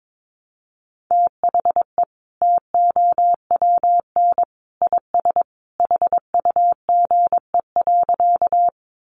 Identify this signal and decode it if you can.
Morse code, heard as a keyed tone, about 22 wpm, 700 Hz. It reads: THE TOWN IS HUGE.
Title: Sentences from Top 1100 Words 22wpm